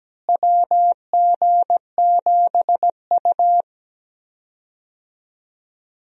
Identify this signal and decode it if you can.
Morse code, heard as a keyed tone, about 17 wpm, 700 Hz. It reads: WG7U
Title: USA Call Signs - 4 Characters - Repeated 3x - 17wpm